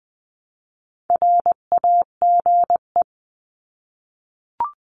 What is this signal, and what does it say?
Morse code, heard as a keyed tone, about 20 wpm, 700 Hz. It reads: RAGE
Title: 259 Lesson 42d - Words 18-20wpm